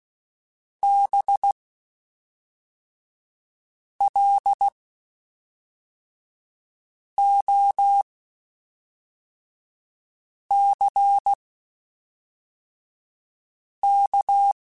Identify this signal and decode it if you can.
Morse code, heard as a keyed tone, about 16 wpm, spaced as for 3 wpm, 800 Hz. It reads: BLOCK
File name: General / General-mp3s / Houses @5